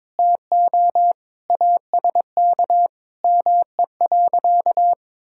Morse code, heard as a keyed tone, about 22 wpm, 700 Hz. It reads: TO ASK ME.
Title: Sentences from Top 1000 Words - Review Part 5 - 22wpm